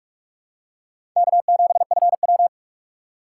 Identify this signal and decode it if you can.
Morse code, heard as a keyed tone, about 44 wpm, 700 Hz. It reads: K7FW
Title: USA Call Signs - 4 Characters - 2x Character Spacing - 45wpm